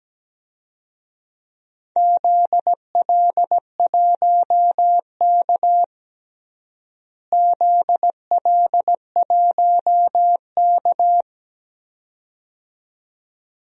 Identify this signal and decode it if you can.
Morse code, heard as a keyed tone, about 17 wpm, 700 Hz. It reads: ZL1K ZL1K
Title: DXCC-100 Call Signs - 4 Characters - Repeated 2x - 17wpm